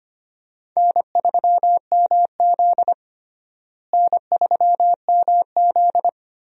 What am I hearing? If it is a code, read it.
Morse code, heard as a keyed tone, about 25 wpm, 700 Hz. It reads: N3MZ N3MZ
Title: USA Call Signs - 4 Characters - Repeated 2x - 25wpm